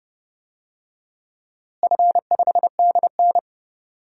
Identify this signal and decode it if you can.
Morse code, heard as a keyed tone, about 30 wpm, 700 Hz. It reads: F5DN
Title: DXCC-100 Call Signs - 4 Characters - Repeated 2x - 30wpm